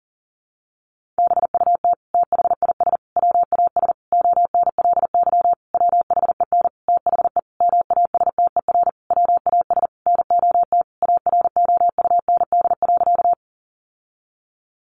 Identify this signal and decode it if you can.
Morse code, heard as a keyed tone, about 40 wpm, 700 Hz. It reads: BUT THIS WAS ONLY WHEN THE MASTER WAS NOT AROUND.
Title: Sentences from Top 1300 Words 40wpm